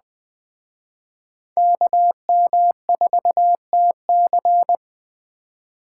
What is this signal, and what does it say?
Morse code, heard as a keyed tone, about 21 wpm, 700 Hz. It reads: KM4TC